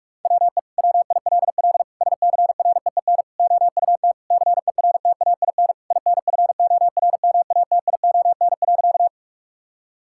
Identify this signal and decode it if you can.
Morse code, heard as a keyed tone, about 46 wpm, 700 Hz. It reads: WE WILL SCREEN OUT CERTAIN INFORMATION.
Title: Sentences from Top 1000 Words - Review Part 6 - 45wpm